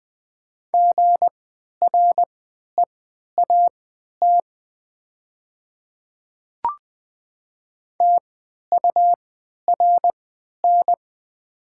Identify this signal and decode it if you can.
Morse code, heard as a keyed tone, about 20 wpm, spaced as for 11 wpm, 700 Hz. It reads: GREAT TURN
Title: 3x Character Spacing - Top 300 Words - 20wpm